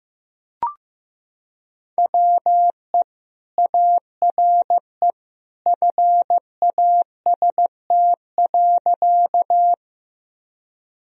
Morse code, heard as a keyed tone, about 15 wpm, 700 Hz. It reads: WE ARE FAST.
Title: Sentences from Top 1000 Words - Review Part 2 - 15wpm